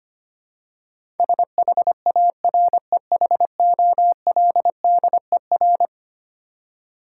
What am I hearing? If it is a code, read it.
Morse code, heard as a keyed tone, about 25 wpm, 700 Hz. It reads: SHAREHOLDER
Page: morse-code-ninja-practice-25wpm